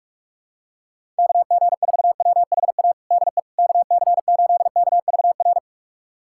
Morse code, heard as a keyed tone, about 44 wpm, 700 Hz. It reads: KG4WHA DE KC8KVR
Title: USA Callsign DE Callsign - 6 Characters - 45wpm